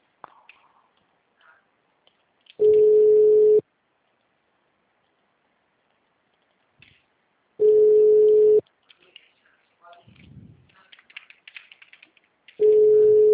Ringback tone (Europe, 425 Hz).
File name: out- -2032- - - .9252